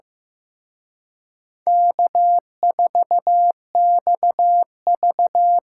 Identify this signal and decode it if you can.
Morse code, heard as a keyed tone, about 17 wpm, 700 Hz. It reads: K4XV
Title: Speed Racing - USA 1x2 Call Signs 15wpm